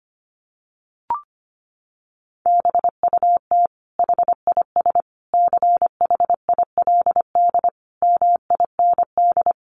Morse code, heard as a keyed tone, about 25 wpm, 700 Hz. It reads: BUT HIS CHILD MIND
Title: Sentences from Top 800 Words 25wpm